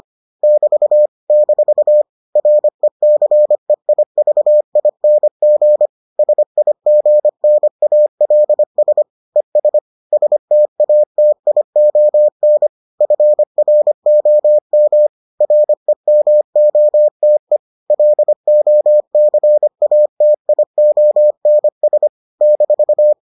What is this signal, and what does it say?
Morse code, heard as a keyed tone, about 25 wpm, 600 Hz. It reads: <BT> - RECEIVING SIGNALS ES STATION FROM REMOTE LOCATIONS -